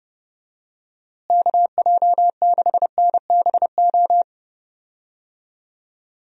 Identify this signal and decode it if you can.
Morse code, heard as a keyed tone, about 31 wpm, 700 Hz. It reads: KJ6NBO
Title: USA Call Signs - 6 Characters - Repeated 2x - 30wpm